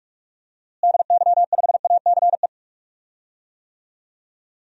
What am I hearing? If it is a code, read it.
Morse code, heard as a keyed tone, about 46 wpm, 690 Hz. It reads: DY5ACE
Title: DXCC-100 Call Signs - 6 Characters - Repeated 2x - 45wpm